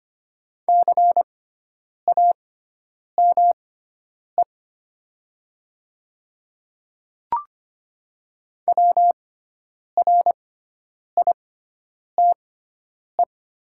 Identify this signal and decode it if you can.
Morse code, heard as a keyed tone, about 25 wpm, spaced as for 9 wpm, 700 Hz. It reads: CAME WRITE
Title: 6x Character Spacing - Top 500 Words - 25wpm